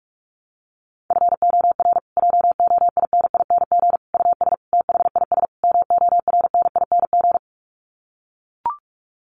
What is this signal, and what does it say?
Morse code, heard as a keyed tone, about 44 wpm, 700 Hz. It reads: FOR JOINING US THIS MORNING
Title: Sets of 5 Words Encore 45wpm